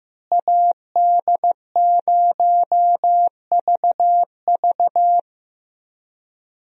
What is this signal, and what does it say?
Morse code, heard as a keyed tone, about 15 wpm, 700 Hz. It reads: AD0VV
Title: USA Call Signs - 5 Characters - Repeated 2x - 15wpm